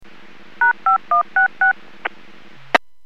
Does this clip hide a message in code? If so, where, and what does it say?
Yes: DTMF from 0.6 to 1.7 s, heard as telephone touch tones, tones of about 130 ms. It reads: #5166